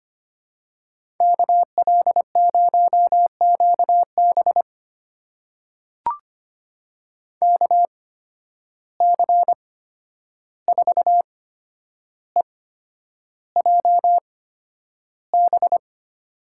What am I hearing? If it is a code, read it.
Morse code, heard as a keyed tone, about 25 wpm, spaced as for 10 wpm, 700 Hz. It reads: KL0QB KC4EJB
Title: USA Call Signs - 8x Character Spacing - 25wpm